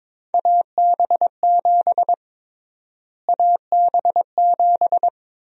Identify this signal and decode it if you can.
Morse code, heard as a keyed tone, about 22 wpm, 700 Hz. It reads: AB7 AB7
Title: USA Call Sign Prefixes - Repeated 2x - 22wpm